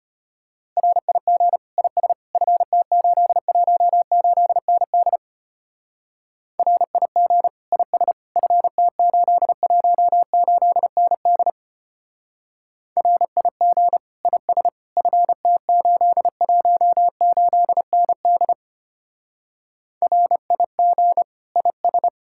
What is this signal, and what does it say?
Morse code, heard as a keyed tone, about 31 wpm, 700 Hz. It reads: RIG IS FT818ND RIG IS FT818ND RIG IS FT818ND RIG IS
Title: Speed-Racing Common QSO Phrases - 25wpm